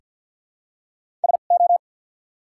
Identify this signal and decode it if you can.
Morse code, heard as a keyed tone, about 48 wpm, spaced as for 35 wpm, 700 Hz. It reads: SO
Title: 2x Character Spacing - Top 100 Words - 50wpm